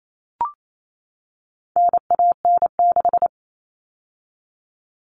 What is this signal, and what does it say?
Morse code, heard as a keyed tone, about 28 wpm, 700 Hz. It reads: NAN6
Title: All Cut Numbers - 4-Digit 28wpm